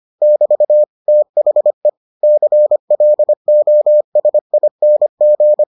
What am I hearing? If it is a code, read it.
Morse code, heard as a keyed tone, about 25 wpm, 600 Hz. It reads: <BT> THE CLOSING